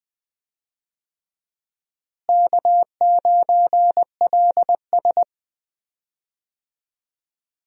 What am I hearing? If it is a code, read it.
Morse code, heard as a keyed tone, about 21 wpm, 700 Hz. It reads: K9LS
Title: USA Call Signs - 4 Characters - Repeated 2x - 20wpm